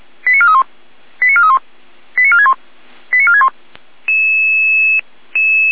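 Five-tone selective call (ZVEI), heard as quick runs of five tones, tones of about 80 ms. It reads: 89431 89431 89461 89461
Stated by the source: ZVEI five-tone call received on SDR.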